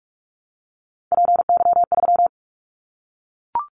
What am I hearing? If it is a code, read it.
Morse code, heard as a keyed tone, about 44 wpm, 700 Hz. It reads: PY3
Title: Callsign Prefixes for Top 100 DXCC - 45wpm